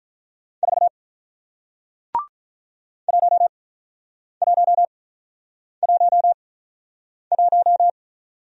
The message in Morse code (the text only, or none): 4 1111